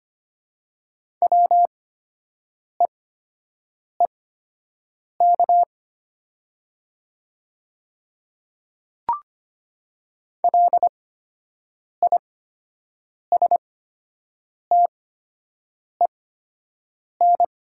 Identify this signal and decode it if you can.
Morse code, heard as a keyed tone, about 25 wpm, spaced as for 7 wpm, 700 Hz. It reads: WEEK LISTEN
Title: 8x Character Spacing - Top 500 Words - 25wpm